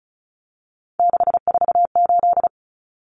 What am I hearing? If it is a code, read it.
Morse code, heard as a keyed tone, about 34 wpm, 700 Hz. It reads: 648
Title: Numbers - 3-Digit 35wpm